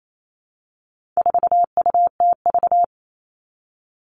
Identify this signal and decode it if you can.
Morse code, heard as a keyed tone, about 28 wpm, 700 Hz. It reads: 4UTV
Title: All Cut Numbers - 4-Digit 28wpm